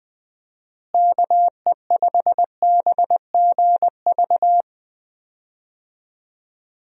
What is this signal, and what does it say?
Morse code, heard as a keyed tone, about 21 wpm, 700 Hz. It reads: KE5BGV
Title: USA Call Signs - 6 Characters - Repeated 2x - 20wpm